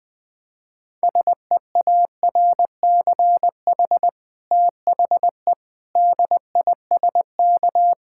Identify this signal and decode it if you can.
Morse code, heard as a keyed tone, about 21 wpm, 700 Hz. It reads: SEARCH THE DISK
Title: Sentences from Top 1000 Words - Review Part 6 - 20wpm